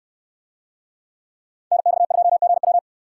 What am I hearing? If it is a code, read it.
Morse code, heard as a keyed tone, about 69 wpm, 700 Hz. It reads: NX2GW